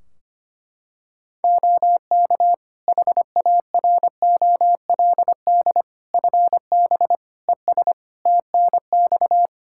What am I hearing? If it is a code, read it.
Morse code, heard as a keyed tone, about 25 wpm, 700 Hz. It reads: OK HAROLD FB ES TNX
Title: Practice QSO - Exchange 2 - Info - 25wpm